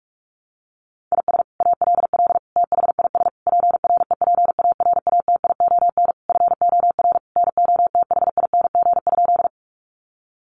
Morse code, heard as a keyed tone, about 44 wpm, 700 Hz. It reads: IS ALL THIS PREPARATION FOR NOTHING?
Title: Sentences from Top 1400 Words 45wpm